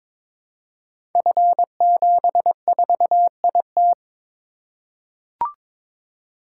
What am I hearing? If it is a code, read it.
Morse code, heard as a keyed tone, about 22 wpm, 700 Hz. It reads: F74IT
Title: Code Groups - 5 Letter:Numbers - 22wpm